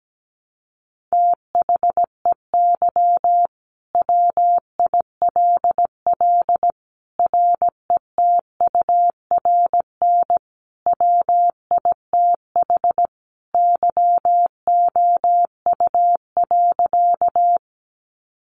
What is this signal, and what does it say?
Morse code, heard as a keyed tone, about 17 wpm, 700 Hz. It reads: THEY WILL RETURN WITH YOU.
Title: Sentences from Top 800 Words 17wpm